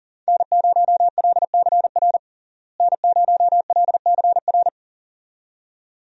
Morse code, heard as a keyed tone, about 40 wpm, 700 Hz. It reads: N0LCR N0LCR
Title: USA Call Signs - 5 Characters - Repeated 2x - 40wpm